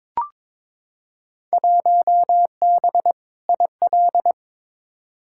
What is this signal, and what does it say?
Morse code, heard as a keyed tone, about 21 wpm, 700 Hz. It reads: 1B IL
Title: Field Day Exchanges - 22wpm